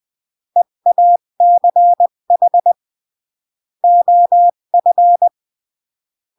Morse code, heard as a keyed tone, about 20 wpm, 700 Hz. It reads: EACH OF
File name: CW / jodorowsky / 82 commandments 20-15